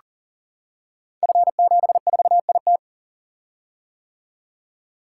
Morse code, heard as a keyed tone, about 42 wpm, 700 Hz. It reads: F74IT
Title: Code Groups - 5 Letter:Numbers - 40wpm